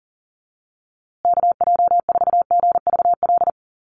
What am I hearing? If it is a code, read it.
Morse code, heard as a keyed tone, about 40 wpm, 700 Hz. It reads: KJ4GVL
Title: USA Call Signs Repeated 3x 40wpm